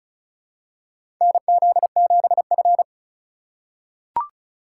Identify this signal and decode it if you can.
Morse code, heard as a keyed tone, about 34 wpm, 700 Hz. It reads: NZ7F
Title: USA Call Signs - 3x Character Spacing - 35wpm